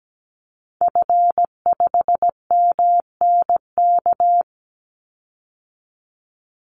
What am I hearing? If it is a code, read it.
Morse code, heard as a keyed tone, about 17 wpm, 700 Hz. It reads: F5MNK
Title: Call Signs - 17wpm